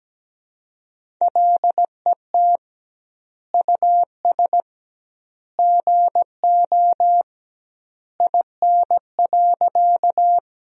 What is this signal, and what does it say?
Morse code, heard as a keyed tone, about 17 wpm, 700 Hz. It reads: LET US GO IN.
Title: 2x Word-Spacing - Sentences from Top 300 Words - 17wpm